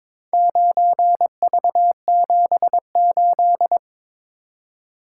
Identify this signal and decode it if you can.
Morse code, heard as a keyed tone, about 21 wpm, 700 Hz. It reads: 9V78
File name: practice / 22wpm / Four Letter-Number Combinations - 22wpm